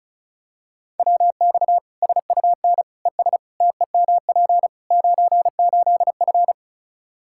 Morse code, heard as a keyed tone, about 36 wpm, 700 Hz. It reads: WX SUN ES TEMP 98F
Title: K1USN SST QSO Practice 35wpm